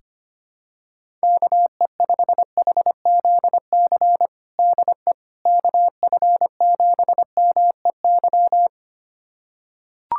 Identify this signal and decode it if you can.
Morse code, heard as a keyed tone, about 25 wpm, 700 Hz. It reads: KE5HZC DE KF7MEY
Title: USA Callsign DE Callsign - 6 Characters - 25wpm